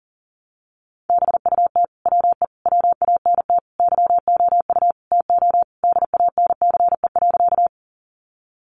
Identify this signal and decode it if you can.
Morse code, heard as a keyed tone, about 38 wpm, 700 Hz. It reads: BUT WE WANT YOU TO DANCE.
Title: Sentences from Top 700 Words 40wpm